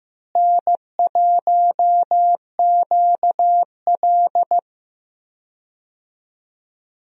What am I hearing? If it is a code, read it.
Morse code, heard as a keyed tone, about 15 wpm, 700 Hz. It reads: N1QL
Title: USA Call Signs - 4 Characters - Repeated 2x - 15wpm